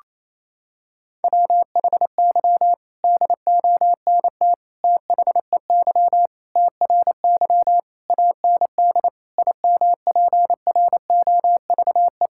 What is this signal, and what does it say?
Morse code, heard as a keyed tone, about 27 wpm, 700 Hz. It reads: WHY DONT THEY TRY AND IMPROVE
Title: Sentences from Top 1000 Words - Review Part 5 - 28wpm